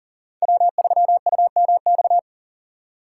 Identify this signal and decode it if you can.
Morse code, heard as a keyed tone, about 40 wpm, 700 Hz. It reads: W3UMX
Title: USA Call Signs - 5 Characters - 6x Character Spacing - 40wpm